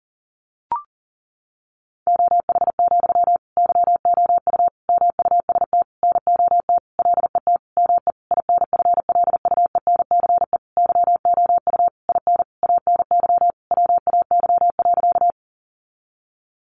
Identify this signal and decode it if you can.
Morse code, heard as a keyed tone, about 40 wpm, 700 Hz. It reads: OH, YOU MUST NOT LET ME INFLUENCE YOU IN ANY WAY.
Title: Sentences from Top 1100 Words 40wpm